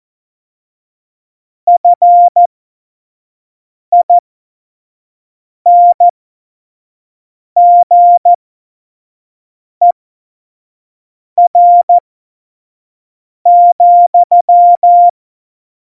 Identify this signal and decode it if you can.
Morse code, heard as a keyed tone, about 14 wpm, spaced as for 5 wpm, 700 Hz. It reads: FINGER,
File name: Morsekurs / Witze / Witz 25. Juni 2019, 25 F70